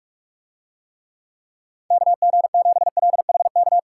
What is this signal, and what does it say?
Morse code, heard as a keyed tone, about 44 wpm, 700 Hz. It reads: KG7LHK